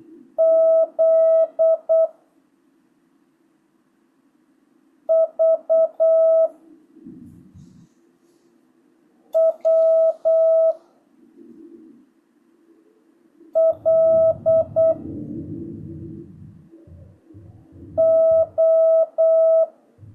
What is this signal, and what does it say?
Morse code, heard as a keyed tone, about 8 wpm, 650 Hz. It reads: Z V W L O
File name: jh8cbh / 2025cw